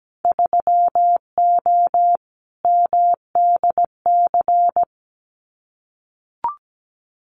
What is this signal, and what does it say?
Morse code, heard as a keyed tone, about 17 wpm, 700 Hz. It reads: 3O MDC
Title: Winter Field Day Exchanges 17wpm